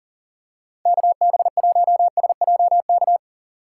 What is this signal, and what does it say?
Morse code, heard as a keyed tone, about 38 wpm, 700 Hz. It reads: KB1SJK
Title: USA Call Signs - 6 Characters - 4x Character Spacing - 40wpm